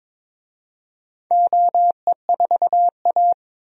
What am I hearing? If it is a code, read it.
Morse code, heard as a keyed tone, about 22 wpm, 700 Hz. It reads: OE4A
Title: DXCC-100 Call Signs - 4 Characters - 22wpm